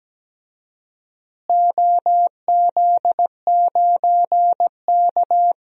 Morse code, heard as a keyed tone, about 17 wpm, 700 Hz. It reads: OZ9K
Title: DXCC-100 Call Signs - 4 Characters - Repeated 3x - 17wpm